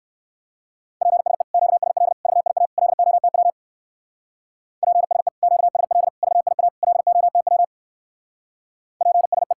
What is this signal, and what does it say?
Morse code, heard as a keyed tone, about 59 wpm, 700 Hz. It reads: PSE QSL VIA LOTW PSE QSL VIA LOTW PSE